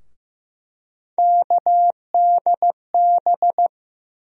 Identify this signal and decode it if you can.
Morse code, heard as a keyed tone, about 15 wpm, 700 Hz. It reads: KDB
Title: USA Call Sign Postfixes - 3 Letters 15wpm